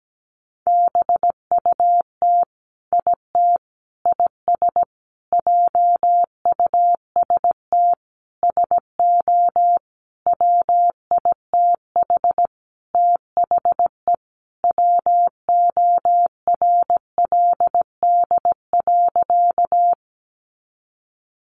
Morse code, heard as a keyed tone, about 17 wpm, 700 Hz. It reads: BUT IT IS JUST SO WITH THE WORLD.
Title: Sentences from Top 200 Words 17wpm